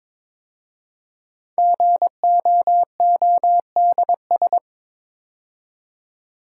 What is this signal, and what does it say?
Morse code, heard as a keyed tone, about 23 wpm, 700 Hz. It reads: GOODS
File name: 5 Letter Words - First 3 Letters Given - 22wpm